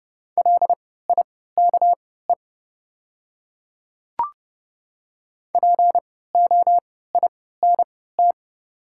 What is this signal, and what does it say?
Morse code, heard as a keyed tone, about 32 wpm, spaced as for 17 wpm, 690 Hz. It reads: LIKE POINT
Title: 3x Character Spacing - Top 200 Words - 30wpm